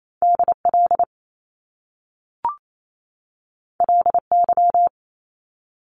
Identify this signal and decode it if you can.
Morse code, heard as a keyed tone, about 28 wpm, 700 Hz. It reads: DL LY
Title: Top Two Letter Combinations in Words 28wpm